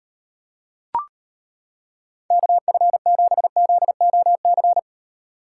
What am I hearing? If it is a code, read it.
Morse code, heard as a keyed tone, about 32 wpm, 700 Hz. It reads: KF7ZOC